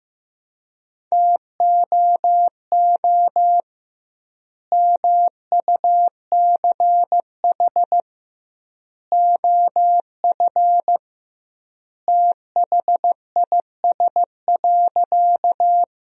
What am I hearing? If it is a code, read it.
Morse code, heard as a keyed tone, about 16 wpm, 700 Hz. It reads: TOO MUCH OF THIS.
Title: 2x Word-Spacing - Sentences from Top 200 Words - 15wpm